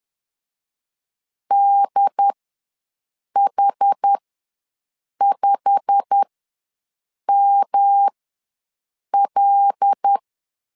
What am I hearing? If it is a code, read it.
Morse code, heard as a keyed tone, about 10 wpm, spaced as for 6 wpm, 800 Hz. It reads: DH5ML
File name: LEZIONI / 9 286 ws 10wpm 4032